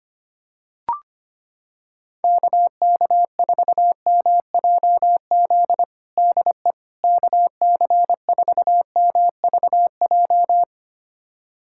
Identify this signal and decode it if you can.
Morse code, heard as a keyed tone, about 25 wpm, 700 Hz. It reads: KK4MJZ DE KC4MVJ
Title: USA Callsign DE Callsign - 6 Characters - 25wpm